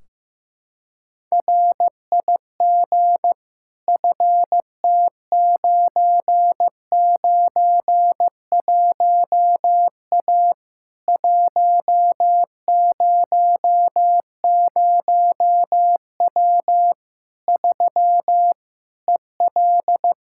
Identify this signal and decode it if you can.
Morse code, heard as a keyed tone, about 15 wpm, 700 Hz. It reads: RIG FT991A 100W 3 EL
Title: K1USN SST QSO Practice 15wpm